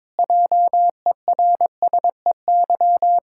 Morse code, heard as a keyed tone, about 21 wpm, 700 Hz. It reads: JERSEY
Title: Practice QSO - Exchange 1 - Beginning - 22wpm